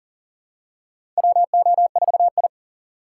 Morse code, heard as a keyed tone, about 40 wpm, 700 Hz. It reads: WO4I
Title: USA Call Signs - 4 Characters - 2x Character Spacing - 40wpm